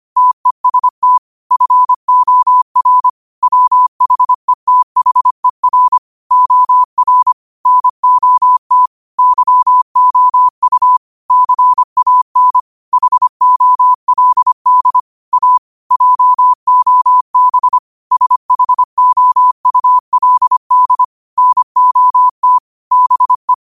Morse code, heard as a keyed tone, about 25 wpm, 1000 Hz. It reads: TEST FOR WHETHER OR NOT YOU CAN HOLD A JOB SHOULD NOT BE
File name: qotdfiles / 25wpm